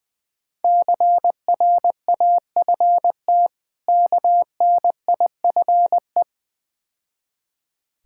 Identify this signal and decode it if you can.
Morse code, heard as a keyed tone, about 23 wpm, 690 Hz. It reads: CRAFT KNIFE
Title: Difficult Words - Speed-Racing with Hints - 20wpm